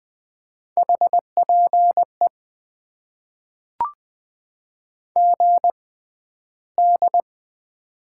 Morse code, heard as a keyed tone, about 20 wpm, spaced as for 12 wpm, 700 Hz. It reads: HPE GD
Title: Top 90 QSO Elements - 6x Character Spacing - 20wpm